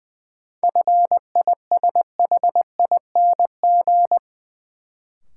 Morse code, heard as a keyed tone, about 20 wpm, 700 Hz. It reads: FISHING
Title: Sentences from Top 1600 Words - New Word Drill Alt - 20wpm